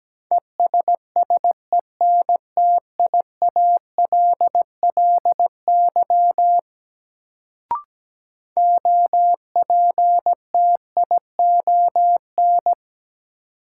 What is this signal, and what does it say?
Morse code, heard as a keyed tone, about 17 wpm, 700 Hz. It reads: ESSENTIALLY OPTION
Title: Sentences from Top 1300 Words - New Word Drill Alt - 17wpm